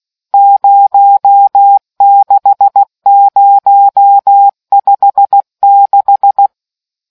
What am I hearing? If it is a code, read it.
Morse code, heard as a keyed tone, about 16 wpm, 800 Hz. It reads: 06056